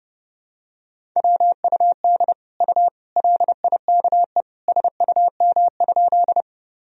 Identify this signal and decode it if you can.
Morse code, heard as a keyed tone, about 32 wpm, 700 Hz. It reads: WUD U LIKE SUM?
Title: Sentences from Top 100 Words and Abbreviations 30wpm